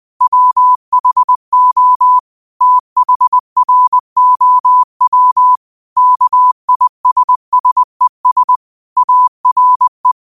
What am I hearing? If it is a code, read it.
Morse code, heard as a keyed tone, about 20 wpm, 1000 Hz. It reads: WHO THROW KISSES ARE
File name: 20wpm